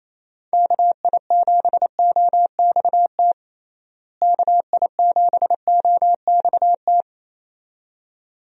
Morse code, heard as a keyed tone, about 27 wpm, 700 Hz. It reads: KI7OXT KI7OXT
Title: USA Call Signs - 6 Characters - Repeated 2x - 28wpm